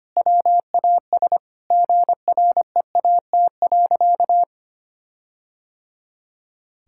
Morse code, heard as a keyed tone, about 25 wpm, 700 Hz. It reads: WAS GREAT.
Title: Sentences from Top 1600 Words 25wpm